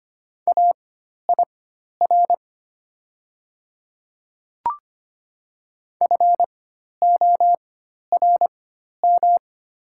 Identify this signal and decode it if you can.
Morse code, heard as a keyed tone, about 25 wpm, spaced as for 12 wpm, 700 Hz. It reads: AIR FORM
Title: 4x Character Spacing - Top 500 Words - 25wpm